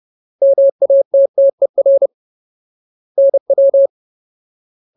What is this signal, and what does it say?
Morse code, heard as a keyed tone, about 30 wpm, 550 Hz. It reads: MATTER NW
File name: Ultimate Word List / 30WPM Ultimate